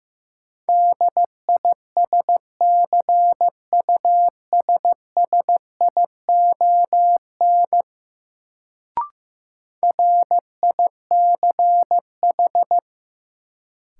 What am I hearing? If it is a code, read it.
Morse code, heard as a keyed tone, about 15 wpm, 700 Hz. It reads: DISCUSSION RICH
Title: Sentences from Top 1100 Words - New Word Drill - 15wpm